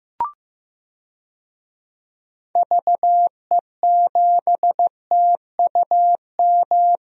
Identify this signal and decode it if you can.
Morse code, heard as a keyed tone, about 15 wpm, 700 Hz. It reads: VE7TUM
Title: DXCC-100 Call Signs - 6 Characters - Repeated 2x - 15wpm